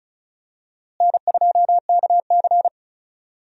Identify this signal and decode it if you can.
Morse code, heard as a keyed tone, about 36 wpm, 700 Hz. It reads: N2KC
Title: USA Call Signs - 4 Characters - Repeated 3x - 35wpm